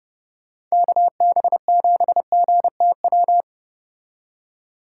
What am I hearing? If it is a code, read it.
Morse code, heard as a keyed tone, about 34 wpm, 700 Hz. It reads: KB7GTW